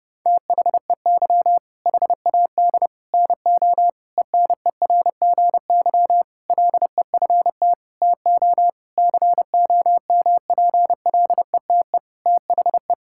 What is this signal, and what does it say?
Morse code, heard as a keyed tone, about 32 wpm, 700 Hz. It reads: THEY HAD NO ENERGY LEFT TO COMPLETE THE
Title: Sentences from Top 1000 Words - Review Part 6 - 30wpm